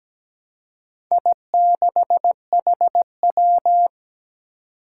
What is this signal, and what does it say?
Morse code, heard as a keyed tone, about 17 wpm, 700 Hz. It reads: I6HW
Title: Four Challenging Letter-Number Combinations 17wpm